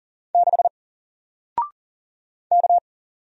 Morse code, heard as a keyed tone, about 40 wpm, 700 Hz. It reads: 6 K
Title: Single Letter-Number Alt 40wpm